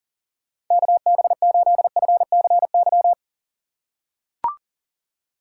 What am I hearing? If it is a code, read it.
Morse code, heard as a keyed tone, about 40 wpm, 690 Hz. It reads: KB8FCY
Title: USA Call Signs - 6 Characters - 2x Character Spacing - 40wpm